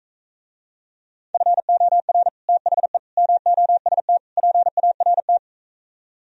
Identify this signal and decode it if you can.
Morse code, heard as a keyed tone, about 45 wpm, 700 Hz. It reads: FOR THE MOST PART